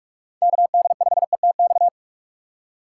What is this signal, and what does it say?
Morse code, heard as a keyed tone, about 46 wpm, 700 Hz. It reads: KD5ETX
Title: USA Call Signs - 2x Character Spacing - 45wpm